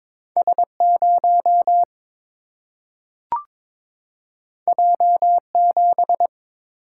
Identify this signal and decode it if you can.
Morse code, heard as a keyed tone, about 22 wpm, 700 Hz. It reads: S 0 J 7
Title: Single-Letter Single-Number 22wpm